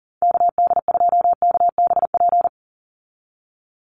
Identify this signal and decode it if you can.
Morse code, heard as a keyed tone, about 40 wpm, 690 Hz. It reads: KD2KBP
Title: USA Call Signs 40wpm